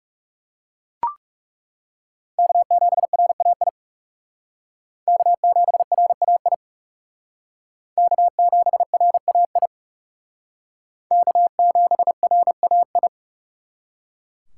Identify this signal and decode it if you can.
Morse code, heard as a keyed tone, about 37 wpm, 700 Hz. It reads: K7RAI K7RAI K7RAI K7RAI
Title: Speed Racing - USA 1x3 Call Signs 30wpm